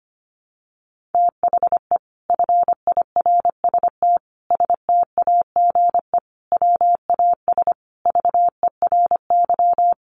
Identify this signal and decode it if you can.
Morse code, heard as a keyed tone, about 25 wpm, 700 Hz. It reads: THE FIRST STAGE WAS VERY
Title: Sentences from Top 1200 Words 25wpm